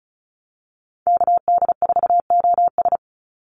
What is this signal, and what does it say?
Morse code, heard as a keyed tone, about 34 wpm, 700 Hz. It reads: KD4OS
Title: Call Signs Repeated 3x - 35wpm